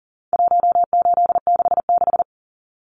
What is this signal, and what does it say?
Morse code, heard as a keyed tone, about 40 wpm, 700 Hz. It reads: 1866
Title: Numbers - 4-Digit 40wpm